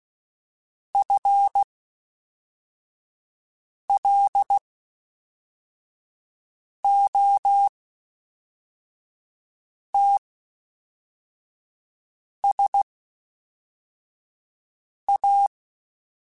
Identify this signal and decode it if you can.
Morse code, heard as a keyed tone, about 16 wpm, spaced as for 4 wpm, 800 Hz. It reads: FLOTSA
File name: Misc / Misc-mp3s / WordPairs sp @5